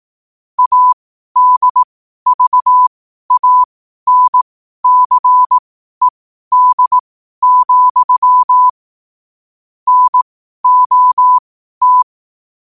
Morse code, heard as a keyed tone, about 18 wpm, spaced as for 12 wpm, 1000 Hz. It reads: ADVANCED, NOT
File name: qotdfiles / 12wpm